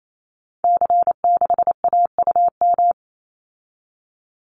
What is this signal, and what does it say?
Morse code, heard as a keyed tone, about 28 wpm, 700 Hz. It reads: C6AUM
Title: Call Signs Repeated 3x - 28wpm